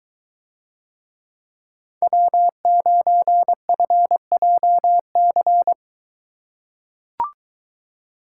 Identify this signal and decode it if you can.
Morse code, heard as a keyed tone, about 22 wpm, 700 Hz. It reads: W9FJC